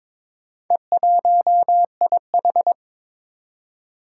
Morse code, heard as a keyed tone, about 22 wpm, 700 Hz. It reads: E1IH
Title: Four Challenging Letter-Number Combinations 22wpm